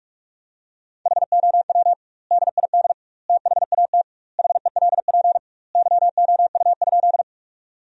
Morse code, heard as a keyed tone, about 41 wpm, 700 Hz. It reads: HOW DID THAT HELP YOU?
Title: 2x Word-Spacing - Sentences from Top 200 Words - 45wpm